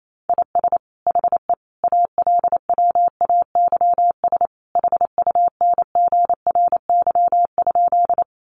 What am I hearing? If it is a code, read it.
Morse code, heard as a keyed tone, about 28 wpm, 700 Hz. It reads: IS HE ALWAYS HUNGRY?
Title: Sentences from Top 1400 Words 28wpm